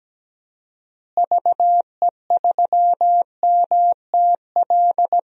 Morse code, heard as a keyed tone, about 17 wpm, 700 Hz. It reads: VE3MTL
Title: DXCC-100 Call Signs - 6 Characters - Repeated 2x - 17wpm